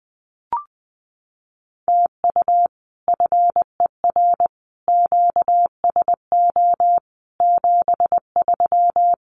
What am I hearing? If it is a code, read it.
Morse code, heard as a keyed tone, about 20 wpm, 700 Hz. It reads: TU FER QSO 73
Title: Common QSO Phrases Encore 20wpm